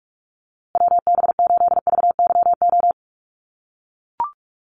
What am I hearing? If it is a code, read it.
Morse code, heard as a keyed tone, about 44 wpm, 690 Hz. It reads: WB8VYO
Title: USA Call Signs Repeated 3x 45wpm